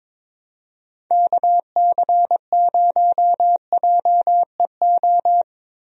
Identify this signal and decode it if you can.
Morse code, heard as a keyed tone, about 22 wpm, 700 Hz. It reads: KC0JEO
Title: USA Call Signs - 6 Characters - Repeated 3x - 22wpm